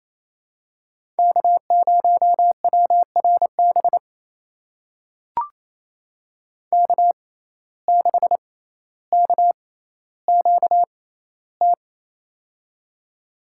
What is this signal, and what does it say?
Morse code, heard as a keyed tone, about 28 wpm, spaced as for 14 wpm, 700 Hz. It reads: K0WRB K6KQT
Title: USA Call Signs - 5 Characters - 6x Character Spacing - 28wpm